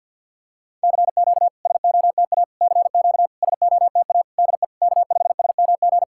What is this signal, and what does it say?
Morse code, heard as a keyed tone, about 52 wpm, 690 Hz. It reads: CQ SOTA CQ SOTA DE K5SMG
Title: SOTA QSO Practice 50wpm